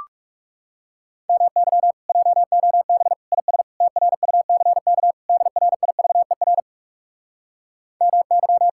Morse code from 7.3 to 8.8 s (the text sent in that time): MY